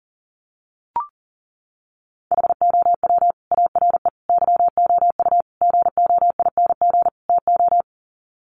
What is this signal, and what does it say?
Morse code, heard as a keyed tone, about 40 wpm, 700 Hz. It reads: HOW ARE YOU GOING TO
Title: Sets of 5 Words 40wpm